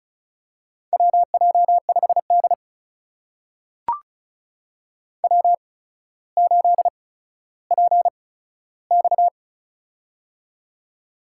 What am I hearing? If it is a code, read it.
Morse code, heard as a keyed tone, about 36 wpm, spaced as for 15 wpm, 700 Hz. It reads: WJ5D W8PX
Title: USA Call Signs - 4 Characters - 8x Character Spacing - 35wpm